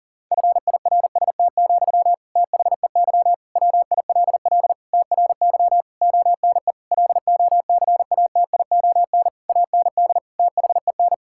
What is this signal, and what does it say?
Morse code, heard as a keyed tone, about 40 wpm, 700 Hz. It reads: FIRST, THEY WILL TRY ONE LOCATION AND THEN
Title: Sentences from Top 1000 Words - Review Part 6 - 40wpm